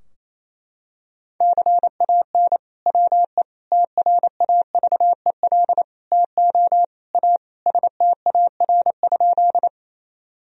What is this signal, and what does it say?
Morse code, heard as a keyed tone, about 28 wpm, 700 Hz. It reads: CAN WE TRAVEL TO A STAR?
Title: Sentences from Top 1000 Words - Review Part 2 - 28wpm